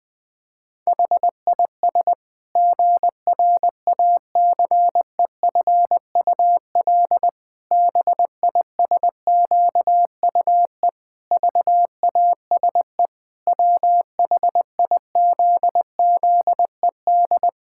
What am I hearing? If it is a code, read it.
Morse code, heard as a keyed tone, about 20 wpm, 700 Hz. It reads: HIS GRACEFUL BISQUE VASE WHIZZED
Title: Pangram Challenge v2 - 20wpm